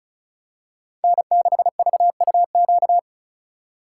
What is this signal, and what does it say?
Morse code, heard as a keyed tone, about 36 wpm, 700 Hz. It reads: N6VUQ
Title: USA Call Signs - 5 Characters - 4x Character Spacing - 35wpm